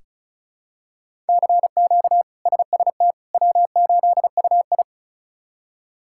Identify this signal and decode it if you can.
Morse code, heard as a keyed tone, about 34 wpm, 700 Hz. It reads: CQ SST W8UI
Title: K1USN SST QSO Practice - 35wpm